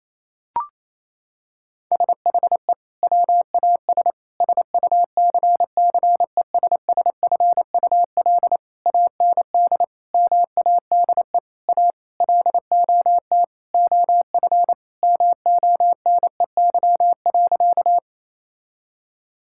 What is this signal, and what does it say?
Morse code, heard as a keyed tone, about 28 wpm, 700 Hz. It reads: SHE WAS SUCCESSFUL AND MADE A LOT OF MONEY.
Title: Sentences from Top 1000 Words - Review Part 6 - 28wpm